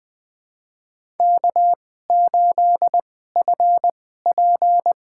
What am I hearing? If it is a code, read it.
Morse code, heard as a keyed tone, about 20 wpm, spaced as for 14 wpm, 700 Hz. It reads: K8FP
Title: USA Call Signs - 2x Character Spacing - 20wpm